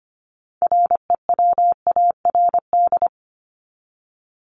Morse code, heard as a keyed tone, about 25 wpm, 700 Hz. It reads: REWARD
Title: Top 1751-2000 Words 25wpm